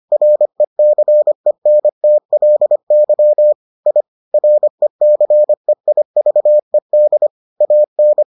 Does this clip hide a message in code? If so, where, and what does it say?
Yes: Morse code from 0.1 to 8.2 s, heard as a keyed tone, about 25 wpm, 600 Hz. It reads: RECENTLY I RECEIVED AN